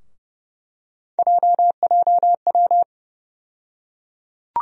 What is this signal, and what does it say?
Morse code, heard as a keyed tone, about 30 wpm, 700 Hz. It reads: JJW
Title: USA Call Sign Postfixes - 3 Letters 30wpm